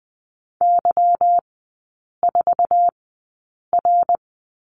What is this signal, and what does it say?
Morse code, heard as a keyed tone, about 20 wpm, spaced as for 8 wpm, 700 Hz. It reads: Y4R
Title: 212 Lesson 34e - Number 8 - Three Characters 20wpm